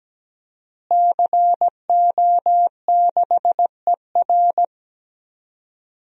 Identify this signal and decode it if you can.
Morse code, heard as a keyed tone, about 17 wpm, 700 Hz. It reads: CO6ER
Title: DXCC-100 Call Signs - 5 Characters - Repeated 3x - 17wpm